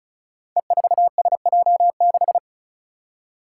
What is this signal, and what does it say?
Morse code, heard as a keyed tone, about 36 wpm, 700 Hz. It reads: E4SJ6
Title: Five Challenging Letter-Number Combinations 35wpm